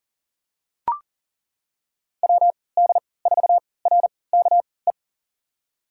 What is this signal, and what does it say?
Morse code, heard as a keyed tone, about 40 wpm, spaced as for 23 wpm, 700 Hz. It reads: WD4RKE
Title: USA Call Signs - 6 Characters - 3x Character Spacing - 40wpm